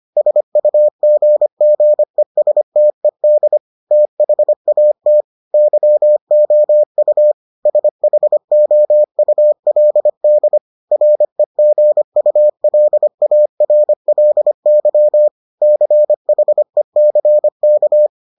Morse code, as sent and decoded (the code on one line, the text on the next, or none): ... ..- --. --. ... - . -.. / - .... .- - / -.-- --- ..- / ... .... --- ..- .-.. -.. / .-. --. ..- .-.. .- .-. .-.. -.-- / -.-. .... -.-. -.-
SUGGESTED THAT YOU SHOULD REGULARLY CHECK